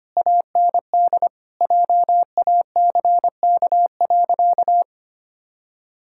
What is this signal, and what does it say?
Morse code, heard as a keyed tone, about 25 wpm, 700 Hz. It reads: AND JACK.
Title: Pangram Challenge v2 - 25wpm